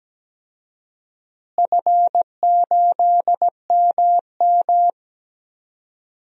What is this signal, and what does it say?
Morse code, heard as a keyed tone, about 17 wpm, 700 Hz. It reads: F8MM
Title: DXCC-100 Call Signs - 4 Characters - Repeated 2x - 17wpm